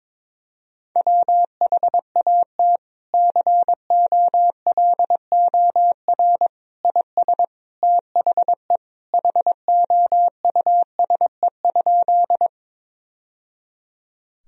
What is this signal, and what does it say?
Morse code, heard as a keyed tone, about 21 wpm, 700 Hz. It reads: WHAT COLOR IS THE HOUSE?
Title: Sentences from Top 1000 Words - Review Part 2 - 22wpm